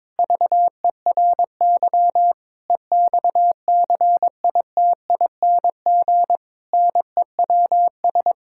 Morse code, heard as a keyed tone, about 22 wpm, 700 Hz. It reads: VERY EXCITING NEWS
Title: Pangram Challenge v2 - 22wpm